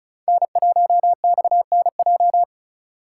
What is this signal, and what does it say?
Morse code, heard as a keyed tone, about 36 wpm, 700 Hz. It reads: N1XNJ
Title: USA Call Signs - 5 Characters - 3x Character Spacing - 35wpm